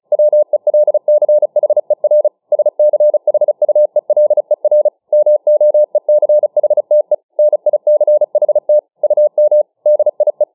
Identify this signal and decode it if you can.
Morse code, heard as a keyed tone, about 35 wpm, 600 Hz. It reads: WELCHER SCHUELER MOECHTE NICHT UM DIE